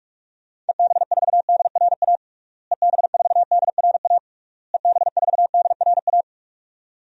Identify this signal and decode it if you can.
Morse code, heard as a keyed tone, about 46 wpm, 700 Hz. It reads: EB4DRA EB4DRA EB4DRA
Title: DXCC-100 Call Signs - 6 Characters - Repeated 3x - 45wpm